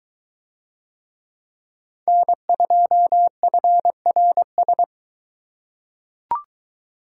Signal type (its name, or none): Morse code